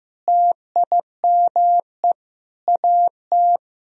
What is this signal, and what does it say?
Morse code, heard as a keyed tone, about 15 wpm, 700 Hz. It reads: TIME AT
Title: Sentences from Top 1000 Words - Review Part 1 - 15wpm